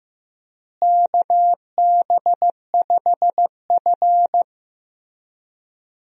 Morse code, heard as a keyed tone, about 16 wpm, 700 Hz. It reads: KB5F
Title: USA Call Signs - 4 Characters - Repeated 3x - 15wpm